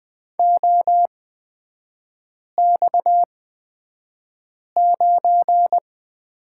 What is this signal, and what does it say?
Morse code, heard as a keyed tone, about 20 wpm, spaced as for 5 wpm, 700 Hz. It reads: OX9
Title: 250 Lesson 40e - BK - Three Characters 20wpm